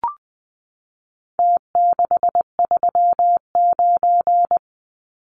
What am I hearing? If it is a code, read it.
Morse code, heard as a keyed tone, about 20 wpm, 700 Hz. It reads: T639
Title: Numbers - Contest Style - 4-Digit 20wpm